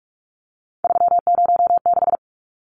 Morse code, heard as a keyed tone, about 44 wpm, 700 Hz. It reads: 306
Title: Numbers - 3-Digit 45wpm